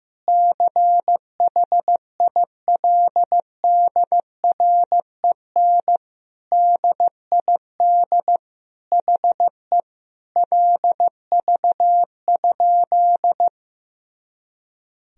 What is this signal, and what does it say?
Morse code, heard as a keyed tone, about 15 wpm, 700 Hz. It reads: CHILDREN DID HE LV?
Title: Sentences from Top 400 Words and Abbreviations 15wpm